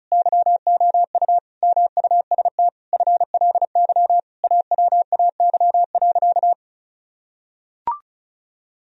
Morse code, heard as a keyed tone, about 34 wpm, 700 Hz. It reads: YOU MUST FLY AWAY.
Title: Sentences from Top 1000 Words - Review Part 2 - 35wpm